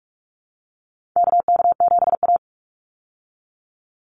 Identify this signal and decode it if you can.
Morse code, heard as a keyed tone, about 44 wpm, 700 Hz. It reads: KK7A
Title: Call Signs Repeated 2x - 45wpm